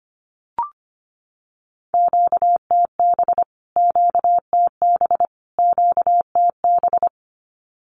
Morse code, heard as a keyed tone, about 25 wpm, 700 Hz. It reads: QTB QTB QTB
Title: Q-Codes Repeated 3x 25wpm